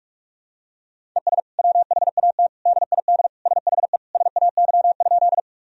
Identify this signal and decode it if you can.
Morse code, heard as a keyed tone, about 46 wpm, 700 Hz. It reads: ES WHAT DID SHE SAY?
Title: Sentences from Top 200 Words and Abbreviations 45wpm